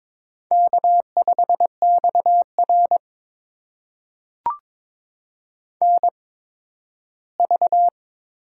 Morse code, heard as a keyed tone, about 22 wpm, spaced as for 13 wpm, 700 Hz. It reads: K5XR NV
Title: USA Call Signs - 4 Characters - 8x Character Spacing - 22wpm